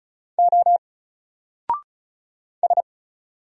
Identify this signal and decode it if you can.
Morse code, heard as a keyed tone, about 33 wpm, 700 Hz. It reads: O S
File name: Single-Letter-Number-35wpm